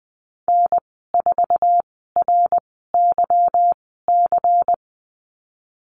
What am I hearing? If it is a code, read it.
Morse code, heard as a keyed tone, about 20 wpm, spaced as for 15 wpm, 700 Hz. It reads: N4RYC
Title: 193 Lesson 31g - Letter v - Call Signs 20wpm